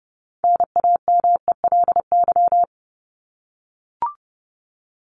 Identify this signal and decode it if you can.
Morse code, heard as a keyed tone, about 30 wpm, 700 Hz. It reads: NAMELY
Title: LY Suffix - Top 100 Words with Base Words 30wpm